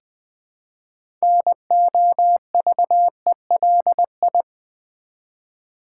Morse code, heard as a keyed tone, about 20 wpm, 700 Hz. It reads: NOVELI
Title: Build A Word - 3 Letters At A Time - 8-Letter Words - 20wpm